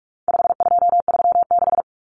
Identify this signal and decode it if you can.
Morse code, heard as a keyed tone, about 44 wpm, 700 Hz. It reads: 5236
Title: Numbers - 4-Digit 45wpm